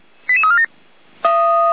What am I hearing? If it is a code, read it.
Five-tone selective call (ZVEI), heard as quick runs of five tones, tones of about 70 ms. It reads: 88347